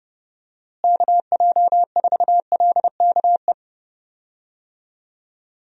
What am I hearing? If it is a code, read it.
Morse code, heard as a keyed tone, about 30 wpm, 700 Hz. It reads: KJ4LKE
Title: USA Call Signs - 6 Characters - Repeated 2x - 30wpm